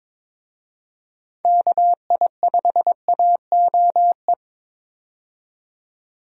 Morse code, heard as a keyed tone, about 23 wpm, 700 Hz. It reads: KI5AOE
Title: USA Call Signs - 6 Characters - 22wpm